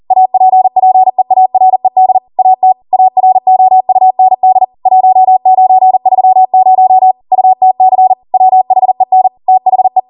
Morse code, heard as a keyed tone, about 40 wpm, 750 Hz. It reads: APPEARED AT AROUND 1930 UTC WHEN THE